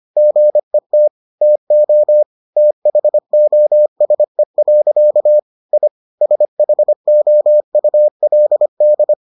Morse code, heard as a keyed tone, about 25 wpm, 600 Hz. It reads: GET TO THOSE. I SHOULD